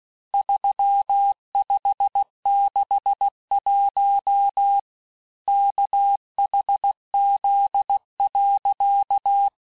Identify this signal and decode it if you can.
Morse code, heard as a keyed tone, about 16 wpm, 800 Hz. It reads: 3561 KHZ.
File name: cours-du-lundi files / Cours-1586-texte-chiffre-vitesse-15mpm